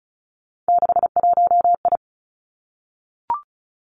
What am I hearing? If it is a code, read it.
Morse code, heard as a keyed tone, about 36 wpm, 700 Hz. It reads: 61I
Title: Three Challenging Letter-Number Combinations 35wpm